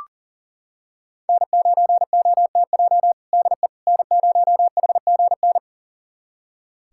Morse code, heard as a keyed tone, about 40 wpm, 700 Hz. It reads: N9OTJ DE N0HGN
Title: USA Callsign DE Callsign - 5 Characters - 40wpm